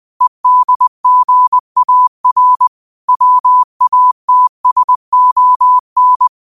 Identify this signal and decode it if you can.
Morse code, heard as a keyed tone, about 20 wpm, 1000 Hz. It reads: EDGAR WATSON
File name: qotdfiles / 20wpm